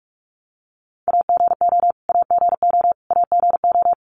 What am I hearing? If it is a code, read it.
Morse code, heard as a keyed tone, about 44 wpm, 700 Hz. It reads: AGO AGO AGO
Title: Top 500 Words Repeated 45wpm